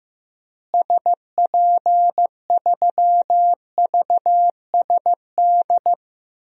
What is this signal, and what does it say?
Morse code, heard as a keyed tone, about 15 wpm, 700 Hz. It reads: SP3VSD
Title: DXCC-100 Call Signs - 6 Characters - Repeated 3x - 15wpm